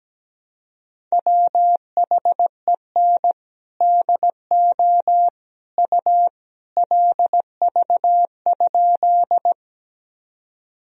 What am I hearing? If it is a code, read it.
Morse code, heard as a keyed tone, about 17 wpm, 700 Hz. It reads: WHEN DO U LV?
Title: Sentences from Top 400 Words and Abbreviations 17wpm